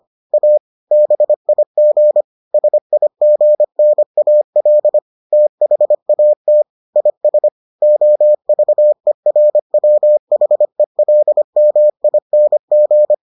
Morse code, heard as a keyed tone, about 25 wpm, 600 Hz. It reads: A BIG SIGNAL THAT IS OVERWHELMING